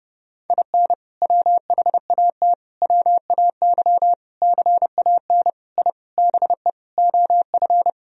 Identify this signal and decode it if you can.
Morse code, heard as a keyed tone, about 30 wpm, 700 Hz. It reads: IN WHAT WAY CAN I BE OF
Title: Sentences from Top 1000 Words - Review Part 4 - 30wpm